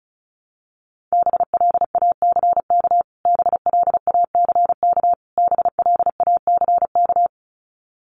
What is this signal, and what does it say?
Morse code, heard as a keyed tone, about 34 wpm, 700 Hz. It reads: BLACK BLACK BLACK
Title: Top 500 Words Repeated 35wpm